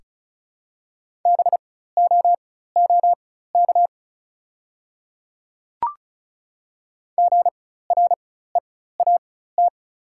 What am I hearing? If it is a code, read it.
Morse code, heard as a keyed tone, about 36 wpm, spaced as for 16 wpm, 690 Hz. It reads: BOOK GREAT
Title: 4x Character Spacing - Top 500 Words - 35wpm